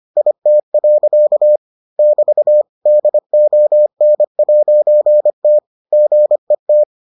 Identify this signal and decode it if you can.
Morse code, heard as a keyed tone, about 25 wpm, 600 Hz. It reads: IT. <BT> DON'T GET